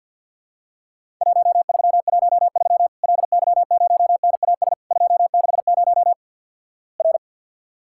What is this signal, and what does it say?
Morse code, heard as a keyed tone, about 53 wpm, 700 Hz. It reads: 1313 LY0NAS 260 R
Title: Worked All Europe DX Contest - QTC Exchange with Repeats - 50wpm